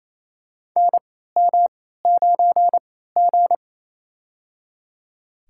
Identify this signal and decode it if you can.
Morse code, heard as a keyed tone, about 28 wpm, spaced as for 16 wpm, 700 Hz. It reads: NM9G
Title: USA Call Signs - 4 Characters - 3x Character Spacing - 28wpm